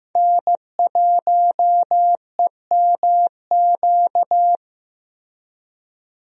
Morse code, heard as a keyed tone, about 15 wpm, 700 Hz. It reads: N1EMQ
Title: USA Call Signs - 5 Characters - Repeated 2x - 15wpm